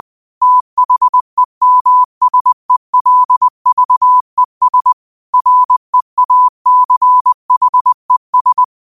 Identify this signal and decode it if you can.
Morse code, heard as a keyed tone, about 20 wpm, 1000 Hz. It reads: THEMSELVES REACHES
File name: qotdfiles / 20wpm